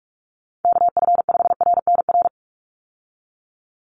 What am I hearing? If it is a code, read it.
Morse code, heard as a keyed tone, about 44 wpm, 700 Hz. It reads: KF5RNR
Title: USA Call Signs 45wpm